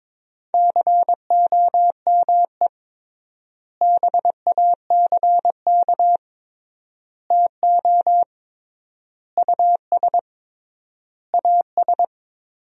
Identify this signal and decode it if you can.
Morse code, heard as a keyed tone, about 21 wpm, 700 Hz. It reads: COME BACK TO US AS
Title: 3x Word-Spacing - Sentences from Top 500 Words - 22wpm